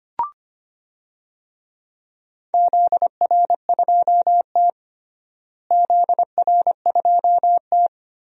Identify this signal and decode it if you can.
Morse code, heard as a keyed tone, about 25 wpm, 700 Hz. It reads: ZR2T ZR2T
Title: DXCC-100 Call Signs - 4 Characters - Repeated 3x - 25wpm